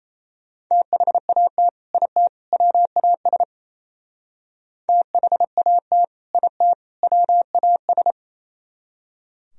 Morse code, heard as a keyed tone, about 36 wpm, 700 Hz. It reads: THAT IT WAS THAT IT WAS